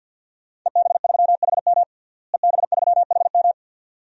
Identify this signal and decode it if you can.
Morse code, heard as a keyed tone, about 52 wpm, 700 Hz. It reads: EB3HM EB3HM
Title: DXCC-100 Call Signs - 5 Characters - Repeated 2x - 50wpm